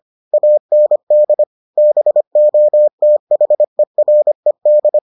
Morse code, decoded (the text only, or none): AND BOTHERED